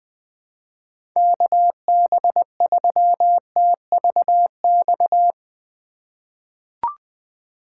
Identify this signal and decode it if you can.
Morse code, heard as a keyed tone, about 21 wpm, 700 Hz. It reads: KB3TVX
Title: USA Call Signs - 6 Characters - 4x Character Spacing - 20wpm